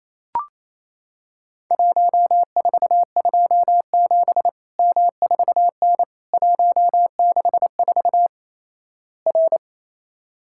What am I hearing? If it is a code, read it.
Morse code, heard as a keyed tone, about 28 wpm, 700 Hz. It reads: 1427 M4N 164 R
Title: Worked All Europe DX Contest - QTC Exchange - 28wpm